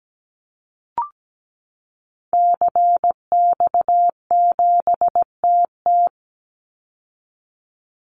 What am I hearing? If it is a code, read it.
Morse code, heard as a keyed tone, about 17 wpm, 700 Hz. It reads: CX7TT
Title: Call Signs Repeated 3x - 17wpm